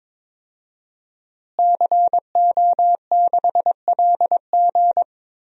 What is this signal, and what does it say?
Morse code, heard as a keyed tone, about 22 wpm, 700 Hz. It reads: CO6LG
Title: DXCC-100 Call Signs - 5 Characters - Repeated 3x - 22wpm